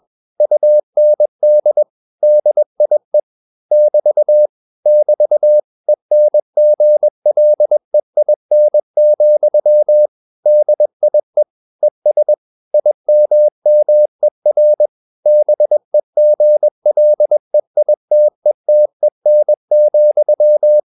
Morse code, heard as a keyed tone, about 21 wpm, 600 Hz. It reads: UND DIE <BT> <BT> ENGLEIN, DIE ES IMMER BEGLEITETEN,